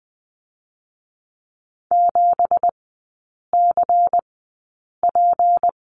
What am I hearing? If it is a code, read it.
Morse code, heard as a keyed tone, about 20 wpm, spaced as for 8 wpm, 700 Hz. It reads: 7CP
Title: 166 Lesson 27e - Number 7 - Three Characters 20wpm